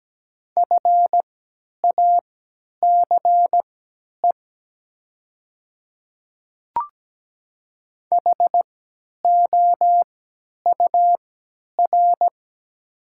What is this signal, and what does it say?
Morse code, heard as a keyed tone, about 17 wpm, spaced as for 10 wpm, 700 Hz. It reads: FACE HOUR
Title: 3x Character Spacing - Top 500 Words - 17wpm